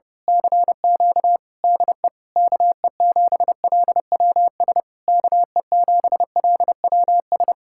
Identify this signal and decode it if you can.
Morse code, heard as a keyed tone, about 30 wpm, 700 Hz. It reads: CQ DE KE7LWS KE7LWS
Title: Speed-Racing Common QSO Phrases - 30wpm